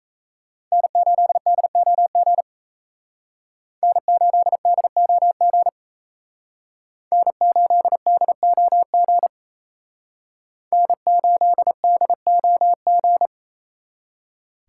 Morse code, heard as a keyed tone, about 34 wpm, 700 Hz. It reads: N8DOG N8DOG N8DOG N8DOG
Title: Speed Racing - USA Call Signs 28wpm